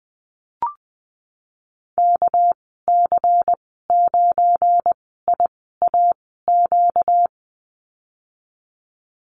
Morse code, heard as a keyed tone, about 20 wpm, spaced as for 14 wpm, 700 Hz. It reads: KC9IAQ
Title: 245 Lesson 39g - Letter z - Call Signs 20wpm